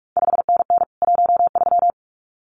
Morse code, heard as a keyed tone, about 46 wpm, 700 Hz. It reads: 5NN 13
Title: CQ WW DX Contest Exchange Encore - 45wpm